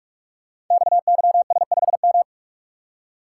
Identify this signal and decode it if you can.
Morse code, heard as a keyed tone, about 44 wpm, 700 Hz. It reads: XYS5M
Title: Code Groups - 5 Letter:Numbers - 45wpm